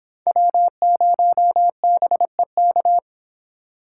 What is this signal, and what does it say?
Morse code, heard as a keyed tone, about 28 wpm, 700 Hz. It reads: W0BEK